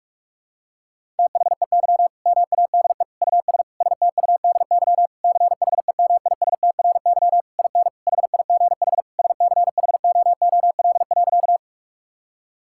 Morse code, heard as a keyed tone, about 46 wpm, 700 Hz. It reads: THEY MADE US STUDY CHEMISTRY IN HIGH SCHOOL.
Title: Sentences from Top 1000 Words - Review Part 6 - 45wpm